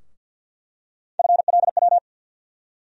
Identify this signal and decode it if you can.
Morse code, heard as a keyed tone, about 47 wpm, 700 Hz. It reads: FLW
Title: USA Call Sign Postfixes - 3 Letters 50wpm